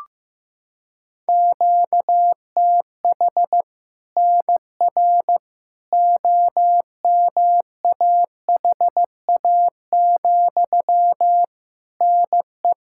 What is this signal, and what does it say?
Morse code, heard as a keyed tone, about 15 wpm, 700 Hz. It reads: QTH NR OMAHA, NE
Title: Practice QSO - Exchange 1 - Beginning - 15wpm